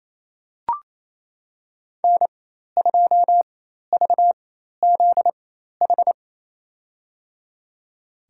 Morse code, heard as a keyed tone, about 27 wpm, spaced as for 13 wpm, 700 Hz. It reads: N2VZH
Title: USA Call Signs - 4x Character Spacing - 28wpm